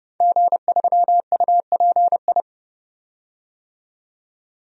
Morse code, heard as a keyed tone, about 30 wpm, 700 Hz. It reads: G3UPI
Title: DXCC-100 Call Signs - 5 Characters - 30wpm